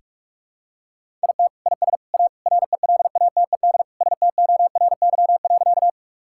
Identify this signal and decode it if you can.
Morse code, heard as a keyed tone, about 46 wpm, 690 Hz. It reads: IT IS A RELATED STORY.
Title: Sentences from Top 1000 Words - Review Part 5 - 45wpm